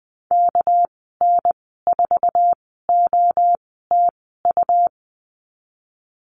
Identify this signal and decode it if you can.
Morse code, heard as a keyed tone, about 20 wpm, spaced as for 14 wpm, 700 Hz. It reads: KN4OTU
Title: 238 Lesson 38g - Letter q - Call Signs 20wpm